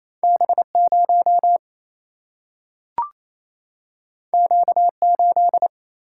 Morse code, heard as a keyed tone, about 28 wpm, 700 Hz. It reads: B0 Q8
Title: Single-Letter Single-Number 28wpm